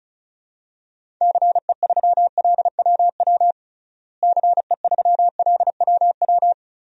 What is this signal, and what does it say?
Morse code, heard as a keyed tone, about 36 wpm, 700 Hz. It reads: CE3LWW CE3LWW
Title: DXCC-100 Call Signs - 6 Characters - Repeated 3x - 35wpm